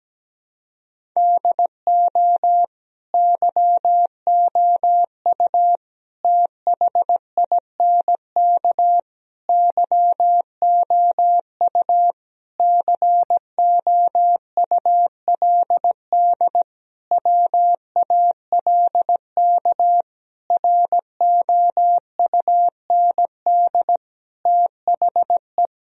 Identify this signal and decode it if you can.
Morse code, heard as a keyed tone, about 17 wpm, 700 Hz. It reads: DO YOU THINK YOU COULD WALK ROUND THE
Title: Sentences from Top 1000 Words - Review Part 1 - 17wpm